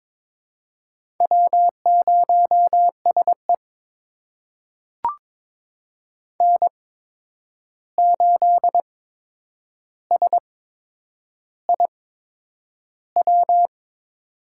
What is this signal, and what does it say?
Morse code, heard as a keyed tone, about 22 wpm, spaced as for 9 wpm, 700 Hz. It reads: W0SE N8SIW
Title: USA Call Signs - 8x Character Spacing - 22wpm